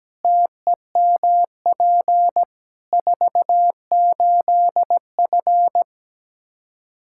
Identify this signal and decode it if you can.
Morse code, heard as a keyed tone, about 17 wpm, 700 Hz. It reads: TEMP 48F
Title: Practice QSO - Exchange 2 - Info - 17wpm